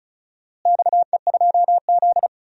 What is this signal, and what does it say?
Morse code, heard as a keyed tone, about 34 wpm, 700 Hz. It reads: XE2Z
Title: DXCC-100 Call Signs - 4 Characters - Repeated 2x - 35wpm